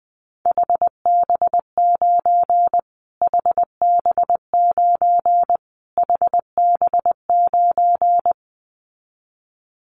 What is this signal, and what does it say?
Morse code, heard as a keyed tone, about 20 wpm, 700 Hz. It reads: HB9 HB9 HB9
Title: Callsign Prefixes for Top 25 DXCC - Repeated 3x - 20wpm